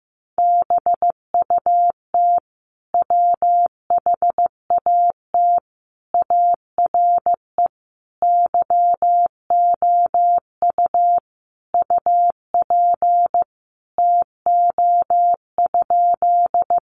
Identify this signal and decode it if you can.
Morse code, heard as a keyed tone, about 15 wpm, 700 Hz. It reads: BUT WHAT ARE YOU UP TO?
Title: Sentences from Top 100 Words 15wpm